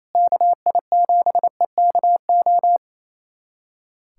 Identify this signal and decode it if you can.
Morse code, heard as a keyed tone, about 28 wpm, 700 Hz. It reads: KI7EKO
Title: USA Callsign DE Callsign 28wpm